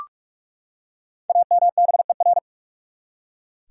Morse code, heard as a keyed tone, about 43 wpm, 700 Hz. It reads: AMBER
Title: USA Common Names - Max 5 Letters - 45wpm